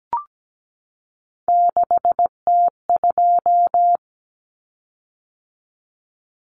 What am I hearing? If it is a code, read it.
Morse code, heard as a keyed tone, about 17 wpm, 700 Hz. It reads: 6T2
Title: Common Cut Numbers - 3-Digit 17wpm